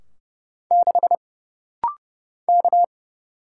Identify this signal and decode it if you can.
Morse code, heard as a keyed tone, about 30 wpm, 700 Hz. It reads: 6 K
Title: Single Letter-Number - Rapid-Fire 30wpm